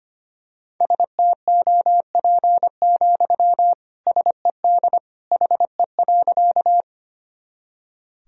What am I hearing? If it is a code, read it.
Morse code, heard as a keyed tone, about 25 wpm, 700 Hz. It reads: STOP, SED HE.
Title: Sentences from Top 300 Words and Abbreviations 25wpm